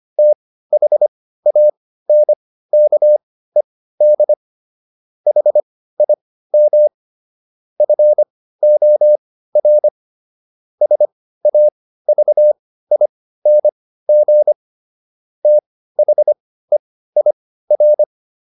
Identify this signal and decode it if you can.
Morse code, heard as a keyed tone, about 25 wpm, spaced as for 15 wpm, 600 Hz. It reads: THANKED HIM FOR SAVING THEIR